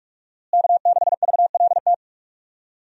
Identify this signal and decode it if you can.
Morse code, heard as a keyed tone, about 43 wpm, 700 Hz. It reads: K6VLT
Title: USA Call Signs - 5 Characters - Repeated 3x - 45wpm